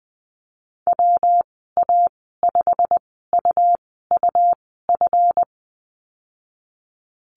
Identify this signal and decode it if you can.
Morse code, heard as a keyed tone, about 20 wpm, spaced as for 15 wpm, 700 Hz. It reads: WA5UUF
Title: 173 Lesson 28g - Number 9 - Call Signs 20wpm